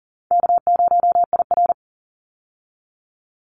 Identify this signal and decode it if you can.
Morse code, heard as a keyed tone, about 40 wpm, 700 Hz. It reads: K0IR
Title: Easy Call Signs - 40wpm